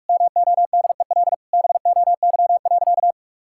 Morse code, heard as a keyed tone, about 41 wpm, 700 Hz. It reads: MODEL BOY.
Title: Sentences from Top 1000 Words - Review Part 5 - 45wpm